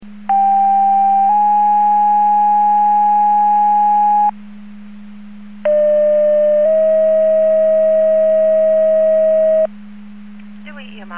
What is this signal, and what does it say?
Two-tone page, heard as a one-second tone, then a three-second tone, about 790 then 830 Hz, about 600 then 640 Hz. It is heard from 0.3 to 9.7 s.